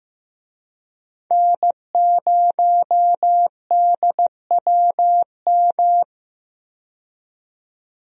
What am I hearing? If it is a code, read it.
Morse code, heard as a keyed tone, about 15 wpm, 700 Hz. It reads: N0DWM
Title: USA Call Signs - 5 Characters - 15wpm